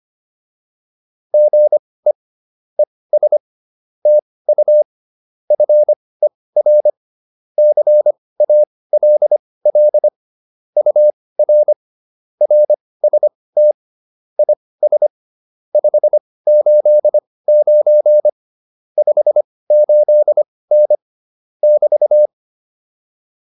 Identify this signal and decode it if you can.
Morse code, heard as a keyed tone, about 25 wpm, spaced as for 18 wpm, 600 Hz. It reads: GE ES TU FER CALL UR RST IS 589 58N <BT>
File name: QSO110 18